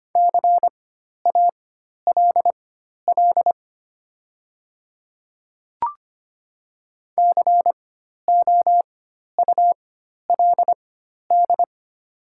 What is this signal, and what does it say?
Morse code, heard as a keyed tone, about 25 wpm, spaced as for 12 wpm, 700 Hz. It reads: CALL COULD
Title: 4x Character Spacing - Top 100 Words - 25wpm